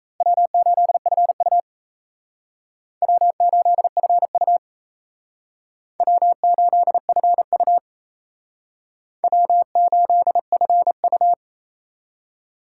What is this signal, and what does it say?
Morse code, heard as a keyed tone, about 35 wpm, 700 Hz. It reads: W8FU W8FU W8FU W8FU
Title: Speed Racing - USA 1x2 Call Signs 28wpm